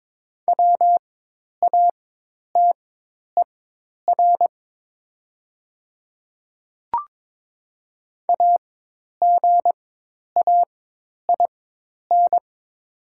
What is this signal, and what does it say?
Morse code, heard as a keyed tone, about 22 wpm, spaced as for 10 wpm, 700 Hz. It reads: WATER AGAIN
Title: 4x Character Spacing - Top 300 Words - 22wpm